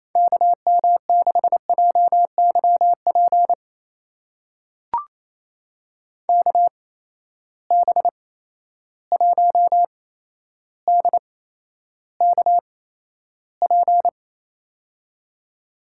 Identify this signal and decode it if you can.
Morse code, heard as a keyed tone, about 28 wpm, spaced as for 12 wpm, 700 Hz. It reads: KM6JYP KB1DKP
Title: USA Call Signs - 6 Characters - 8x Character Spacing - 28wpm